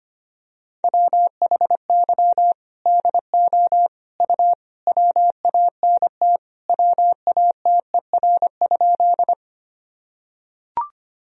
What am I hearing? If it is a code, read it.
Morse code, heard as a keyed tone, about 25 wpm, 700 Hz. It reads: WHY DO U WANT WATER?
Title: Sentences from Top 200 Words and Abbreviations 25wpm